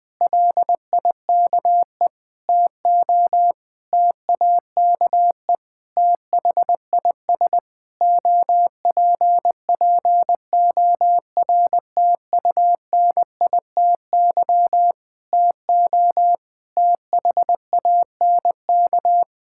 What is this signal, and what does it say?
Morse code, heard as a keyed tone, about 19 wpm, 700 Hz. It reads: LIKE TO TAKE THIS OPPORTUNITY TO THANK
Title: Sentences with 8-Word Phrases - Spoken Beginning - 20wpm